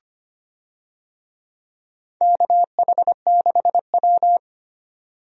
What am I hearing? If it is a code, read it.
Morse code, heard as a keyed tone, about 25 wpm, 700 Hz. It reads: KH6W
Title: DXCC-100 Call Signs - 4 Characters - 25wpm